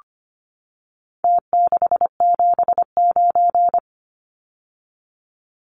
Morse code, heard as a keyed tone, about 25 wpm, 700 Hz. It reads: T679
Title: Numbers - Contest Style - 4-Digit 25wpm